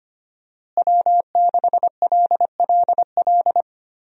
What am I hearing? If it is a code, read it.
Morse code, heard as a keyed tone, about 25 wpm, 700 Hz. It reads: W6LLL
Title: USA Call Signs - 5 Characters - Repeated 3x - 25wpm